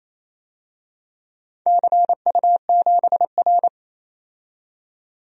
Morse code, heard as a keyed tone, about 28 wpm, 700 Hz. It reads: CU7R
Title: DXCC-100 Call Signs - 4 Characters - 28wpm